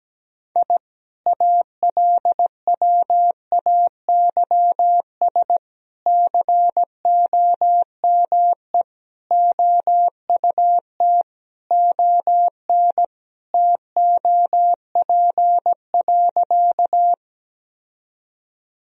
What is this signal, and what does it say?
Morse code, heard as a keyed tone, about 17 wpm, 700 Hz. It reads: I ALWAYS COME OUT ON TOP.
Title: Sentences from Top 1000 Words - Review Part 2 - 17wpm